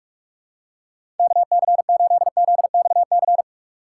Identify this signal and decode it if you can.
Morse code, heard as a keyed tone, about 46 wpm, 700 Hz. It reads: KC8ZXC
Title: USA Call Signs - 6 Characters - 2x Character Spacing - 45wpm